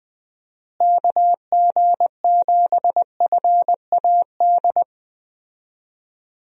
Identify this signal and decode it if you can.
Morse code, heard as a keyed tone, about 20 wpm, 700 Hz. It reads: KG7FAD
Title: USA Call Signs - 6 Characters - Repeated 2x - 20wpm